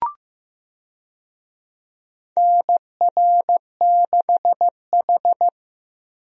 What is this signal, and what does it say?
Morse code, heard as a keyed tone, about 15 wpm, 700 Hz. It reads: NR6H
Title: USA Call Signs - 4 Characters - 15wpm